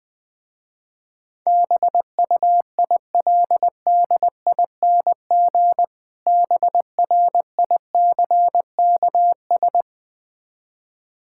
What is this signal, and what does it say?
Morse code, heard as a keyed tone, about 20 wpm, 700 Hz. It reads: BUILDING BRICKS
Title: Difficult Words with Hints - 20wpm